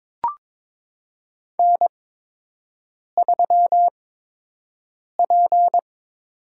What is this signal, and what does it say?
Morse code, heard as a keyed tone, about 22 wpm, spaced as for 6 wpm, 700 Hz. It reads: N3P
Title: USA Call Signs - 4 Characters - 8x Character Spacing - 22wpm